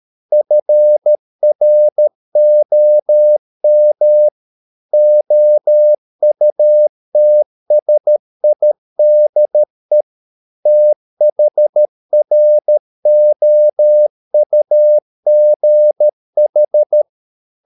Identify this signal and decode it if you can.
Morse code, heard as a keyed tone, about 13 wpm, 600 Hz. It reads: FROM OUTSIDE THROUGH